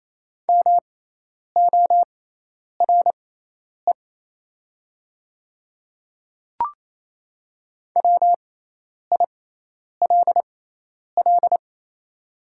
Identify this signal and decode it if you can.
Morse code, heard as a keyed tone, about 28 wpm, spaced as for 10 wpm, 700 Hz. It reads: MORE WILL
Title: 6x Character Spacing - Top 100 Words - 28wpm